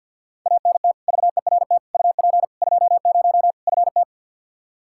Morse code, heard as a keyed tone, about 47 wpm, 700 Hz. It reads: ANT VERT UP 20 FT
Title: Practice QSO - Exchange 2 - Info - 50wpm